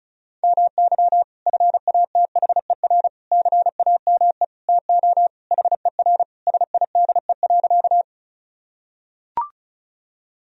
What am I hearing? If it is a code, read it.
Morse code, heard as a keyed tone, about 36 wpm, 700 Hz. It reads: MY FATHER CAME TO HER SIDE.
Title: Sentences from Top 1000 Words - Review Part 1 - 35wpm